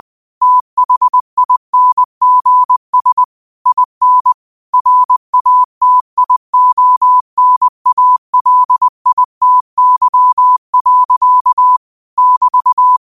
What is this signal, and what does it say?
Morse code, heard as a keyed tone, about 20 wpm, 1000 Hz. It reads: THINGS IN RATIONALITY. <BT>